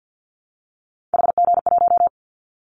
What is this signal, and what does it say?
Morse code, heard as a keyed tone, about 50 wpm, 700 Hz. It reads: HG1
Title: Callsign Prefixes for Top 25 DXCC - Repeated 3x - 50wpm